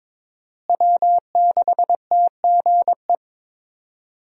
Morse code, heard as a keyed tone, about 22 wpm, 690 Hz. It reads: W6TGE
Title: USA Call Signs - 5 Characters - Repeated 3x - 22wpm